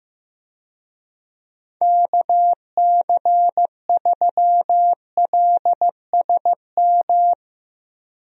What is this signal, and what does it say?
Morse code, heard as a keyed tone, about 16 wpm, 700 Hz. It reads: KC3LSM
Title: USA Call Signs - 6 Characters - Repeated 2x - 15wpm